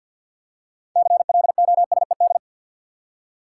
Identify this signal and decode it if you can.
Morse code, heard as a keyed tone, about 52 wpm, 700 Hz. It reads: CLOSED
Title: Six Letter Words - 50wpm